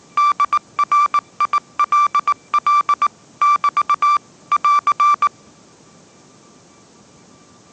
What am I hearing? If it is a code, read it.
Morse code, heard as a keyed tone, about 19 wpm, 1170 Hz. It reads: DRILL <BT> <AR>